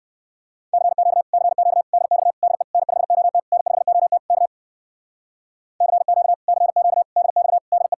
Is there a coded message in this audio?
Yes: Morse code from 0.7 to 7.9 s, heard as a keyed tone, about 63 wpm, 700 Hz. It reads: CQ CQ DX DE N5OT N5OT K CQ CQ DX DE